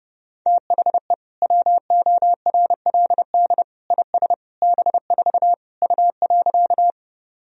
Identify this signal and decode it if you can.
Morse code, heard as a keyed tone, about 30 wpm, 700 Hz. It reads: THE WORLD IS B4 U.
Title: Sentences from Top 200 Words and Abbreviations 30wpm